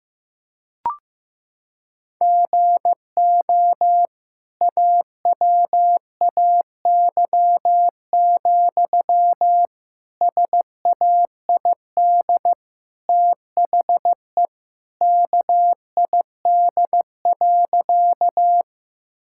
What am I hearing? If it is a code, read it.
Morse code, heard as a keyed tone, about 15 wpm, 700 Hz. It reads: GO AWAY, SAID THE KID.
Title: Sentences from Top 1600 Words 15wpm